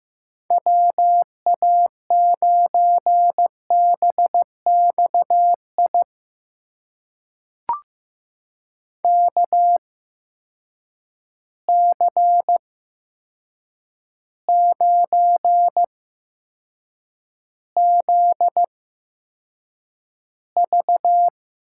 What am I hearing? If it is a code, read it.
Morse code, heard as a keyed tone, about 15 wpm, spaced as for 7 wpm, 700 Hz. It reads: WA9BXI KC9ZV
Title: USA Call Signs - 6 Characters - 8x Character Spacing - 15wpm